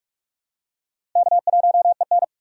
Morse code, heard as a keyed tone, about 45 wpm, 700 Hz. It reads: K1EN
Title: USA Call Signs - 4 Characters - 45wpm